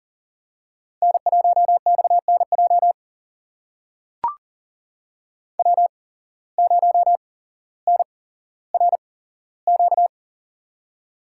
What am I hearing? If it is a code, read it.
Morse code, heard as a keyed tone, about 38 wpm, spaced as for 17 wpm, 700 Hz. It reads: N1XNJ W0NRQ
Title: USA Call Signs - 5 Characters - 8x Character Spacing - 40wpm